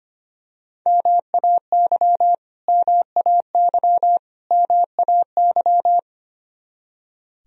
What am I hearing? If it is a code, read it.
Morse code, heard as a keyed tone, about 25 wpm, 700 Hz. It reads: MAY MAY MAY
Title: Top 100 Words Repeated 25wpm